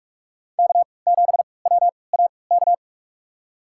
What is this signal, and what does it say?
Morse code, heard as a keyed tone, about 47 wpm, spaced as for 26 wpm, 700 Hz. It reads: K7WAK
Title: USA Call Signs - 5 Characters - 3x Character Spacing - 45wpm